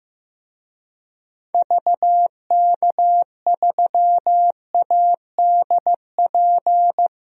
Morse code, heard as a keyed tone, about 15 wpm, 700 Hz. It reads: VK3ADP
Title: DXCC-100 Call Signs - 6 Characters - 15wpm